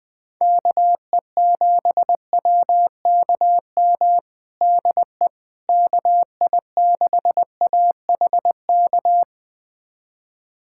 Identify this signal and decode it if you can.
Morse code, heard as a keyed tone, about 21 wpm, 700 Hz. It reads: KE7WKM DE KI6AHK
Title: USA Callsign DE Callsign - 6 Characters - 20wpm